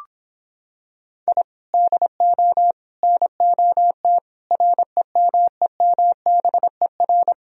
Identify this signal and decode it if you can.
Morse code, heard as a keyed tone, about 26 wpm, 700 Hz. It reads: I DO NOT REMEMBER